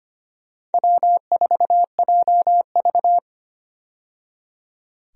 Morse code, heard as a keyed tone, about 25 wpm, 700 Hz. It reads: W4JV
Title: USA Call Signs - 4 Characters - 25wpm